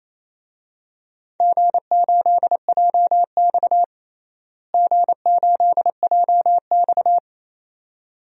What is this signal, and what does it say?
Morse code, heard as a keyed tone, about 28 wpm, 700 Hz. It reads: G8JX G8JX
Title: DXCC-100 Call Signs - 4 Characters - Repeated 2x - 28wpm